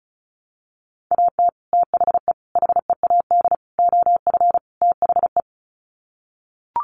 Morse code, heard as a keyed tone, about 34 wpm, 700 Hz. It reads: AT THE HEAD OF THE
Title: Sets of 5 Words 35wpm